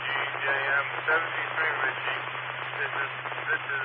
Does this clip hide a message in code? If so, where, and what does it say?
Yes: Emergency Alert System (SAME) tones from 0.4 to 3.1 s, heard as harsh buzzing bursts, with speech.